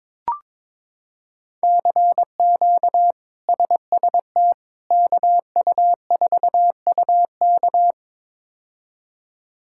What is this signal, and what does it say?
Morse code, heard as a keyed tone, about 22 wpm, 700 Hz. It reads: CQ SST KU4UK
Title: K1USN SST QSO Practice - 22wpm